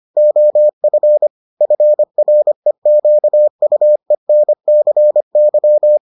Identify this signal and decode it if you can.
Morse code, heard as a keyed tone, about 25 wpm, 600 Hz. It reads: OF FREQUENCY